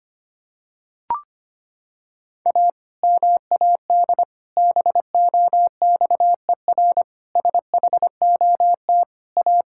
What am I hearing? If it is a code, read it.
Morse code, heard as a keyed tone, about 25 wpm, 700 Hz. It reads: A MAD BOXER SHOT A
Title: Pangram Challenge v2 - 25wpm